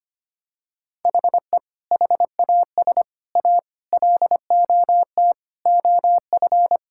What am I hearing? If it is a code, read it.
Morse code, heard as a keyed tone, about 25 wpm, 700 Hz. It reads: HE HAS A LOT OF
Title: Sentences from Top 1000 Words - Review Part 6 - 25wpm